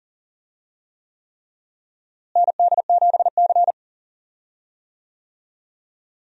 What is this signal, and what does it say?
Morse code, heard as a keyed tone, about 40 wpm, 700 Hz. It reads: ND7C
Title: USA Call Signs - 4 Characters - Repeated 3x - 40wpm